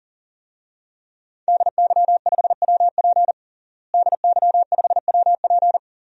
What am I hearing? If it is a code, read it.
Morse code, heard as a keyed tone, about 40 wpm, 700 Hz. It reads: DY5WP DY5WP
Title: DXCC-100 Call Signs - 5 Characters - Repeated 3x - 40wpm